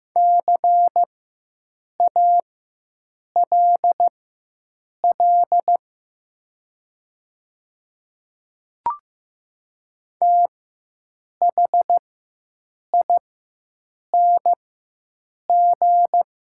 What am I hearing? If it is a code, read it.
Morse code, heard as a keyed tone, about 15 wpm, spaced as for 7 wpm, 700 Hz. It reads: CALL THING
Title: 4x Character Spacing - Top 200 Words - 15wpm